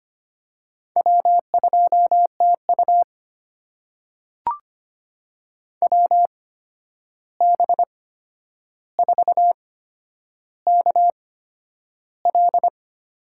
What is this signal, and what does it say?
Morse code, heard as a keyed tone, about 25 wpm, spaced as for 10 wpm, 700 Hz. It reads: W2TU WB4KL
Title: USA Call Signs - 8x Character Spacing - 25wpm